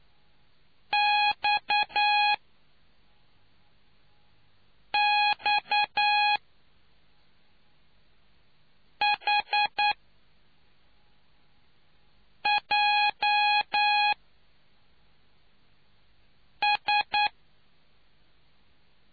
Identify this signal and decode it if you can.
Morse code, heard as a keyed tone, about 9 wpm, spaced as for 3 wpm, 830 Hz. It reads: XXHJS